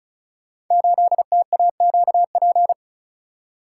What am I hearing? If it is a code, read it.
Morse code, heard as a keyed tone, about 33 wpm, 690 Hz. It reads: 8TAQP
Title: Code Groups - 5 Letter:Numbers - 35wpm